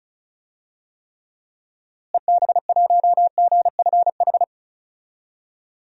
Morse code, heard as a keyed tone, about 34 wpm, 700 Hz. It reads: EB1GFH
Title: DXCC-100 Call Signs - 6 Characters - Repeated 3x - 35wpm